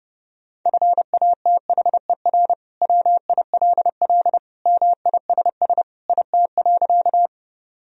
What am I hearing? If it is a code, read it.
Morse code, heard as a keyed tone, about 29 wpm, 700 Hz. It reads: FATHER WILL MISS IT.
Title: Sentences from Top 1000 Words - Review Part 3 - 30wpm